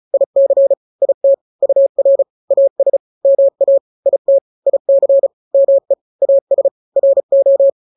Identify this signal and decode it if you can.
Morse code, heard as a keyed tone, about 34 wpm, 550 Hz. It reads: IC IT UR AS MA IT IC ME AS RO
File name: n-grams / 2-grams 35 30 WPM